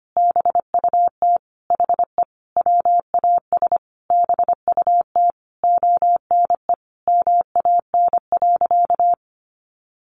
Morse code, heard as a keyed tone, about 25 wpm, 700 Hz. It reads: BUT HE WAS BUT ONE MAN.
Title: Sentences from Top 1500 Words 25wpm